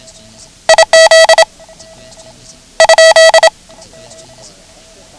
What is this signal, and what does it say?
Morse code, heard as a keyed tone, about 27 wpm, 650 Hz. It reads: IZ ?
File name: ham radio / CodeQuick / Code Quick 09